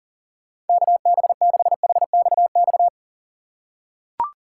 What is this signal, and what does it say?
Morse code, heard as a keyed tone, about 40 wpm, 700 Hz. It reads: KB6HXX
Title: USA Call Signs - 6 Characters - 4x Character Spacing - 40wpm